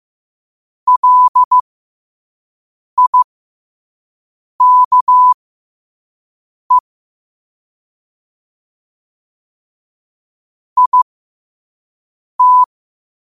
Morse code, heard as a keyed tone, about 15 wpm, spaced as for 5 wpm, 1000 Hz. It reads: LIKE IT
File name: qotdfiles / 5wpm